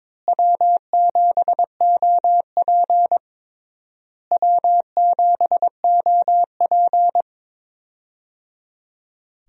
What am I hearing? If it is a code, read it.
Morse code, heard as a keyed tone, about 22 wpm, 700 Hz. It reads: W7OP W7OP
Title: USA Call Signs - 4 Characters - Repeated 3x - 22wpm